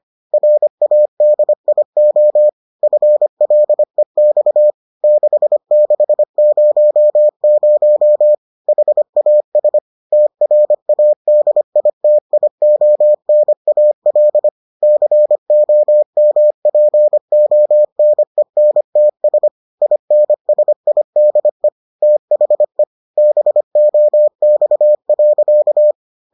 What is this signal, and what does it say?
Morse code, heard as a keyed tone, about 25 wpm, 600 Hz. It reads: RADIO FLEX 6600 HAS TRADITIONAL COMPONENTS INSIDE THE BOX.